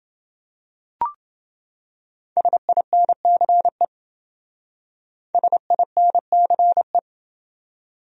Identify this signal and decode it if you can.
Morse code, heard as a keyed tone, about 28 wpm, 700 Hz. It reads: SINCE SINCE